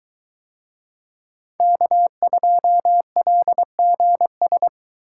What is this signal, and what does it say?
Morse code, heard as a keyed tone, about 22 wpm, 700 Hz. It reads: K2LGS